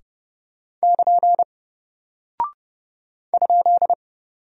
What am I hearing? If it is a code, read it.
Morse code, heard as a keyed tone, about 30 wpm, 700 Hz. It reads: <KN> ?
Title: Single Character (Letter:Number:Punctuation:Prosign) Alt 30wpm